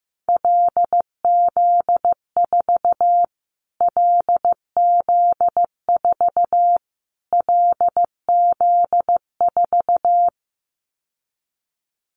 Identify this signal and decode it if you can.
Morse code, heard as a keyed tone, about 15 wpm, 700 Hz. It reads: LZ4 LZ4 LZ4
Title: Callsign Prefixes for Top 25 DXCC - Repeated 3x - 15wpm